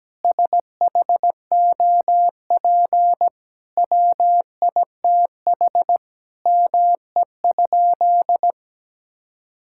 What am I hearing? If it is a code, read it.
Morse code, heard as a keyed tone, about 17 wpm, 700 Hz. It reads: SHOP WITH ME?
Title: Sentences from Top 1600 Words 17wpm